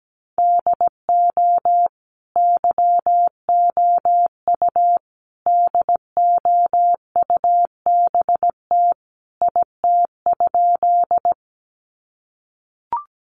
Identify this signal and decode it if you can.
Morse code, heard as a keyed tone, about 17 wpm, 700 Hz. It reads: DO YOU DOUBT IT?
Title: Sentences from Top 1500 Words 17wpm